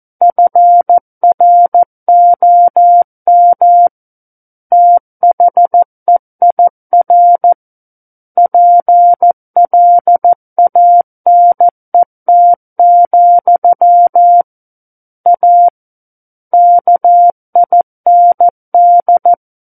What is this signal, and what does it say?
Morse code, heard as a keyed tone, about 14 wpm, 700 Hz. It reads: FROM THEIR PLANET, A KIND
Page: War of the Worlds - 05-Chapter 5 - 14 WPM